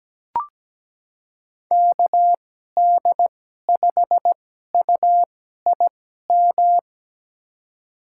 Morse code, heard as a keyed tone, about 17 wpm, spaced as for 12 wpm, 700 Hz. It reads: KD5UIM
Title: USA Call Signs - 6 Characters - 2x Character Spacing - 17wpm